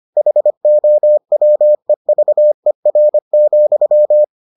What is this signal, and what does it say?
Morse code, heard as a keyed tone, about 25 wpm, 600 Hz. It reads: HOWEVER,